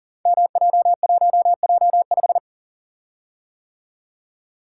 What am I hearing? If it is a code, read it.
Morse code, heard as a keyed tone, about 39 wpm, 700 Hz. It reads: MJ1J5
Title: Code Groups - 5 Letter:Numbers - 40wpm